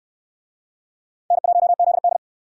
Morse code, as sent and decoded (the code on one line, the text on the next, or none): -. .---- --.. -..
N1ZD